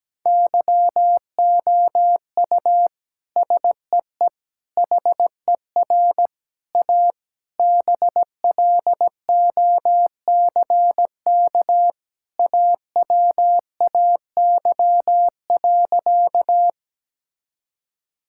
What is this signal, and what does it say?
Morse code, heard as a keyed tone, about 17 wpm, 700 Hz. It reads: YOU SEE HER A BLOCK AWAY.
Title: Sentences from Top 1600 Words 17wpm